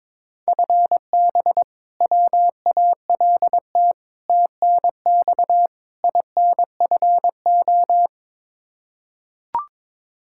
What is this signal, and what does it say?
Morse code, heard as a keyed tone, about 21 wpm, 700 Hz. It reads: FB WALT TNX INFO
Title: Practice QSO - Exchange 3 - More Info - 22wpm